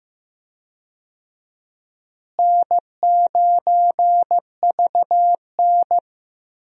Morse code, heard as a keyed tone, about 15 wpm, 700 Hz. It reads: N9VN
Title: USA Call Signs - 4 Characters - Repeated 3x - 15wpm